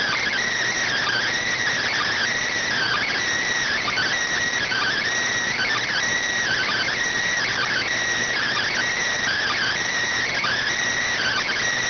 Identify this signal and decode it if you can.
Emergency Alert System (SAME) tones (header bursts), heard as harsh buzzing bursts. It is heard from 0.8 to 11.6 s.